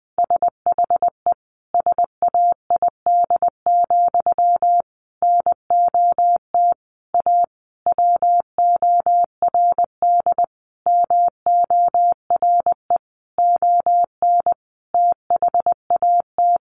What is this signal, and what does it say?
Morse code, heard as a keyed tone, about 20 wpm, 700 Hz. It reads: SHE SAID, NOT A WORD MORE ON THAT
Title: Sentences from Top 800 Words 20wpm